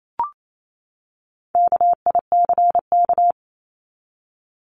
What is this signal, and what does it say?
Morse code, heard as a keyed tone, about 28 wpm, 700 Hz. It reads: KICK
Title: Top 1251-1500 Words 28wpm